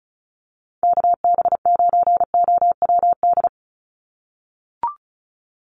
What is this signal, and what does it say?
Morse code, heard as a keyed tone, about 34 wpm, 700 Hz. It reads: KB9OWD
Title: Call Signs - 35wpm